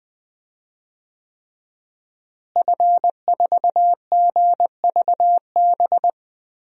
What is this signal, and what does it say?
Morse code, heard as a keyed tone, about 21 wpm, 700 Hz. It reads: F4GVB
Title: DXCC-100 Call Signs - 5 Characters - Repeated 3x - 20wpm